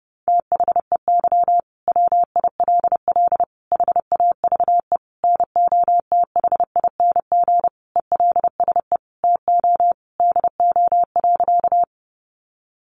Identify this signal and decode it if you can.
Morse code, heard as a keyed tone, about 30 wpm, 700 Hz. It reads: THEY WILL HAVE NOTHING ELSE TO DO.
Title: Sentences from Top 800 Words 30wpm